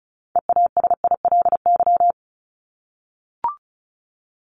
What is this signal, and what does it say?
Morse code, heard as a keyed tone, about 34 wpm, 700 Hz. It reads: EASILY
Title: Top 751-1000 Words 35wpm